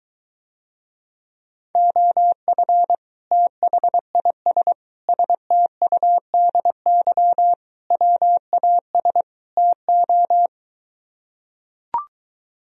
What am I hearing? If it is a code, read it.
Morse code, heard as a keyed tone, about 23 wpm, 700 Hz. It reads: OF THIS STUDY WAS TO